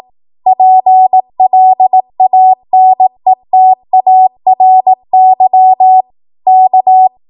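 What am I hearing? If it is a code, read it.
Morse code, heard as a keyed tone, about 18 wpm, 750 Hz. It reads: PLANETARY K
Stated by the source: ARRL 18 wpm Morse code practice archive transmission.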